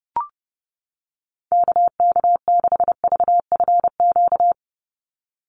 Morse code, heard as a keyed tone, about 30 wpm, 700 Hz. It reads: KK6VFQ
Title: USA Call Signs 30wpm